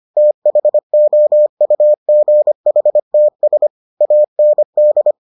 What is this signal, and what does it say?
Morse code, heard as a keyed tone, about 25 wpm, 600 Hz. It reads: THOUGHTS AND